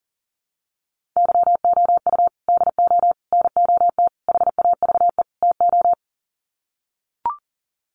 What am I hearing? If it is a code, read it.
Morse code, heard as a keyed tone, about 40 wpm, 700 Hz. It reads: YOU DO NOT HAVE TO
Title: Sets of 5 Words 40wpm